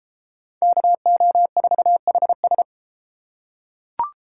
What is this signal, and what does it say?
Morse code, heard as a keyed tone, about 31 wpm, 700 Hz. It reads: KO4HS